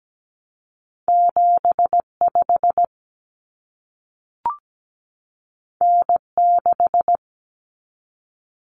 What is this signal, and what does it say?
Morse code, heard as a keyed tone, about 17 wpm, 700 Hz. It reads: 75 N6
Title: Common Cut Numbers - 2-Digit 17wpm